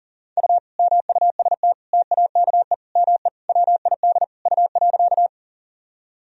Morse code, heard as a keyed tone, about 40 wpm, 700 Hz. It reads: U MUST TAKE ME WID U.
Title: Sentences from Top 200 Words and Abbreviations 40wpm